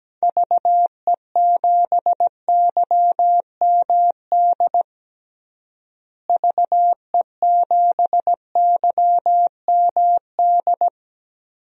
Morse code, heard as a keyed tone, about 17 wpm, 700 Hz. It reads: VE7YMD VE7YMD
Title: DXCC-100 Call Signs - 6 Characters - Repeated 2x - 17wpm